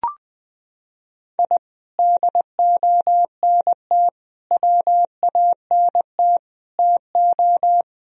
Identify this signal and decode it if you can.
Morse code, heard as a keyed tone, about 20 wpm, 700 Hz. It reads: I DONT WANT TO
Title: Sentences from Top 1000 Words - Review Part 4 - 20wpm